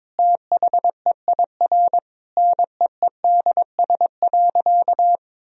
Morse code, heard as a keyed tone, about 22 wpm, 700 Hz. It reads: THEIR NEEDS.
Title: Sentences from Top 1000 Words - Review Part 3 - 22wpm